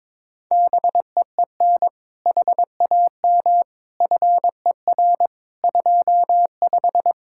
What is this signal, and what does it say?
Morse code, heard as a keyed tone, about 22 wpm, 700 Hz. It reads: BEEN HAM FER 25
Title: Practice QSO - Exchange 3 - More Info - 22wpm